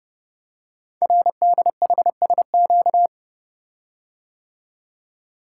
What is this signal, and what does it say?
Morse code, heard as a keyed tone, about 28 wpm, 700 Hz. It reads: RDHSQ
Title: Code Groups - 5 Letters - 30wpm